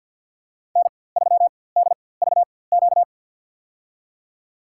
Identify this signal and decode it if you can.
Morse code, heard as a keyed tone, about 53 wpm, spaced as for 24 wpm, 700 Hz. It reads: N3DVQ
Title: USA Call Signs - 5 Characters - 4x Character Spacing - 50wpm